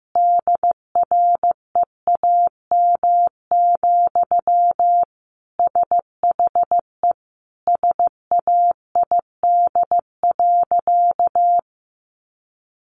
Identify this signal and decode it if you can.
Morse code, heard as a keyed tone, about 14 wpm, 700 Hz. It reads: DREAM, SHE SAID.
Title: Sentences from Top 1500 Words 15wpm